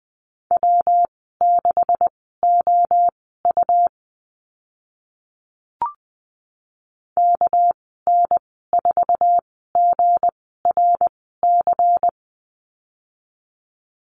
Morse code, heard as a keyed tone, about 20 wpm, spaced as for 14 wpm, 700 Hz. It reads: W6OU KN4GRC
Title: 213 Lesson 34g - Number 8 - Call Signs 20wpm